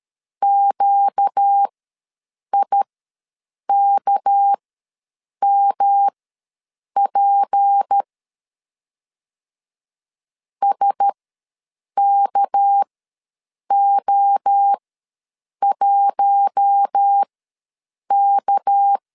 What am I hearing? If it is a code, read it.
Morse code, heard as a keyed tone, about 13 wpm, spaced as for 7 wpm, 800 Hz. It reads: QIKMP SKO1K
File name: LEZIONI / 6 PKX0 / 6 PK ws 12wpm 3116